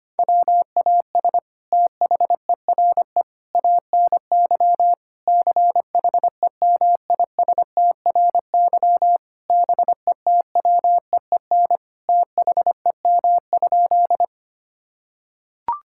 Morse code, heard as a keyed tone, about 25 wpm, 700 Hz. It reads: WAS THERE ANY CHEMISTRY BETWEEN THEM?
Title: Sentences from Top 1000 Words - Review Part 6 - 25wpm